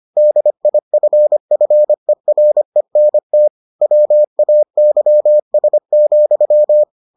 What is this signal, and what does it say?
Morse code, heard as a keyed tone, about 25 wpm, 600 Hz. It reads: DIFFERENT WAYS,